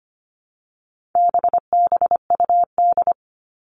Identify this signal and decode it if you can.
Morse code, heard as a keyed tone, about 25 wpm, 700 Hz. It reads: BBUD
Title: All Cut Numbers - 4-Digit 25wpm